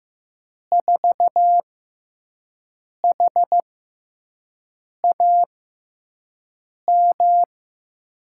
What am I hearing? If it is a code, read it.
Morse code, heard as a keyed tone, about 15 wpm, spaced as for 5 wpm, 700 Hz. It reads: 4HAM
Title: USA Call Signs - 6x Character Spacing - 15wpm